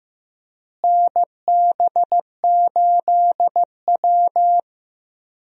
Morse code, heard as a keyed tone, about 15 wpm, 700 Hz. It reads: NB8W
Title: USA Call Signs - 4 Characters - Repeated 2x - 15wpm